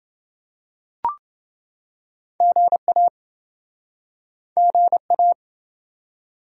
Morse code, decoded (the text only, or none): GA GA